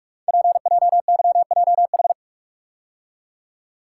Morse code, heard as a keyed tone, about 44 wpm, 700 Hz. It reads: PJYJH
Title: Code Groups - 5 Letters - 45wpm